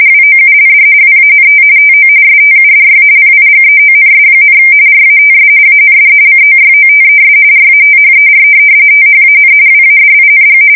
RTTY, about 45.45 baud. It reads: WOULD WANT TO STUDY PIGEONS, RATS AND COYOTES RATHER THAN WADING
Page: 170 Shift 2125/2295